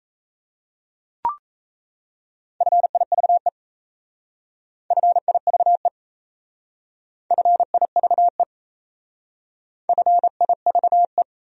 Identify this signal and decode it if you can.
Morse code, heard as a keyed tone, about 33 wpm, 700 Hz. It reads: FIVE FIVE FIVE FIVE
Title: Speed-Racing Top 500 Words 28wpm